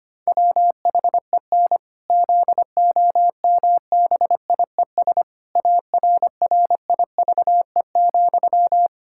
Morse code, heard as a keyed tone, about 25 wpm, 700 Hz. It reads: WHEN ZOMBIES ARRIVE,
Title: Pangram Challenge v2 - 25wpm